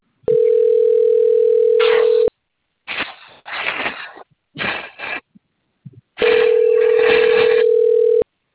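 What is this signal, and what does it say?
Ringback tone (North America), 440 + 480 Hz.
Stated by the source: Outbound telephone call recorded on the Sao Carlos phone system.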